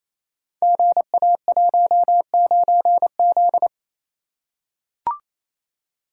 Morse code, heard as a keyed tone, about 27 wpm, 700 Hz. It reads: GA19Z
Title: Code Groups - 5 Letter:Numbers - 28wpm